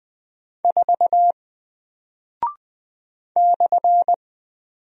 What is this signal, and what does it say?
Morse code, heard as a keyed tone, about 20 wpm, 700 Hz. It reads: 4 /
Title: Single Character (Letter:Number:Punctuation:Prosign) Alt 20wpm